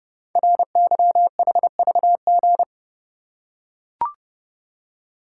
Morse code, heard as a keyed tone, about 28 wpm, 700 Hz. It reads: RYHVG
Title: Code Groups - 5 Letters - 30wpm